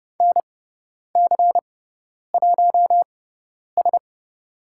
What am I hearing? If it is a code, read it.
Morse code, heard as a keyed tone, about 30 wpm, spaced as for 10 wpm, 700 Hz. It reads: NC1S
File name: Lesson 16g - Letter c - Call Signs 30wpm